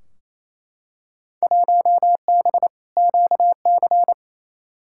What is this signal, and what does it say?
Morse code, heard as a keyed tone, about 28 wpm, 700 Hz. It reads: 1B QC
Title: Field Day Exchanges Encore - 28wpm